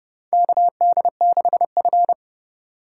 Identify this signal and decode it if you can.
Morse code, heard as a keyed tone, about 30 wpm, 700 Hz. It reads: KD6F
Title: USA Call Signs - 3x Character Spacing - 30wpm